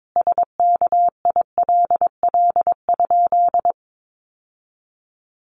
Morse code, heard as a keyed tone, about 22 wpm, 700 Hz. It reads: SKILL?
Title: Sentences from Top 1200 Words 22wpm